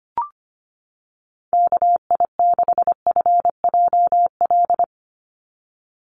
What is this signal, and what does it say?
Morse code, heard as a keyed tone, about 25 wpm, 700 Hz. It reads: KI6FJL
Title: USA Call Signs 25wpm